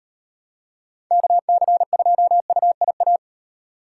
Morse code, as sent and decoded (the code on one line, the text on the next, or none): -.- -.-. ..--- ..- .. .-
KC2UIA